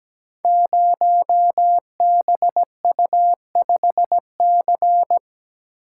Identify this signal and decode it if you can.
Morse code, heard as a keyed tone, about 17 wpm, 700 Hz. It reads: 0BU5C
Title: Code Groups - 5 Letter:Numbers - 17wpm